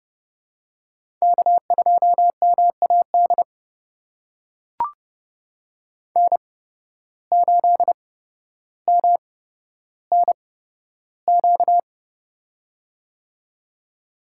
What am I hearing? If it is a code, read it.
Morse code, heard as a keyed tone, about 32 wpm, spaced as for 13 wpm, 700 Hz. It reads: K2MAD N8MNQ
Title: USA Call Signs - 5 Characters - 8x Character Spacing - 30wpm